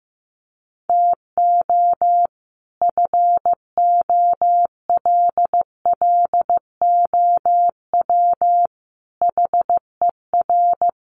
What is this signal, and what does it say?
Morse code, heard as a keyed tone, about 14 wpm, 700 Hz. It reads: TO FOLLOW HER
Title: Sentences from Top 1100 Words 15wpm